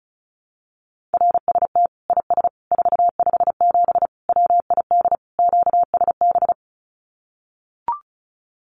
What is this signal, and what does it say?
Morse code, heard as a keyed tone, about 34 wpm, 700 Hz. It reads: RST IS 457 WID QSB
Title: Common QSO Phrases 35wpm